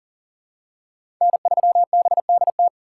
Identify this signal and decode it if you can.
Morse code, heard as a keyed tone, about 40 wpm, 700 Hz. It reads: N3BDT
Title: USA Call Signs - 5 Characters - 6x Character Spacing - 40wpm